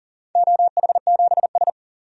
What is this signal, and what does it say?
Morse code, heard as a keyed tone, about 37 wpm, 700 Hz. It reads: OH7S
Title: DXCC-100 Call Signs - 4 Characters - 40wpm